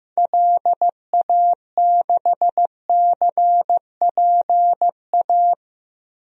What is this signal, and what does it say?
Morse code, heard as a keyed tone, about 16 wpm, 700 Hz. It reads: LA6CPA
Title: DXCC-100 Call Signs - 6 Characters - Repeated 2x - 15wpm